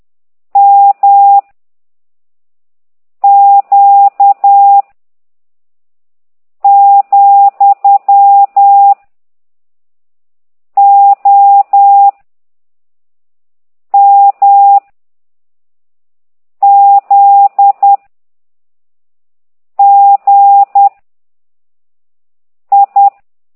Morse code, heard as a keyed tone, about 10 wpm, 800 Hz. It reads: M Q , O M Z G I
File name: ct2idw / exerc13